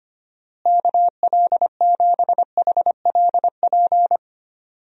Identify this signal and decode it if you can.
Morse code, heard as a keyed tone, about 25 wpm, 700 Hz. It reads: KL7HLP
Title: DXCC-100 Call Signs - 6 Characters - Repeated 3x - 25wpm